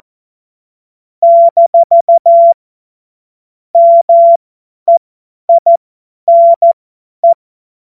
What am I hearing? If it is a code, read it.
Morse code, heard as a keyed tone, about 14 wpm, spaced as for 10 wpm, 680 Hz. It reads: - MEINE
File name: Morsekurs / Witze / Witz 20 Februar 2024 50 F70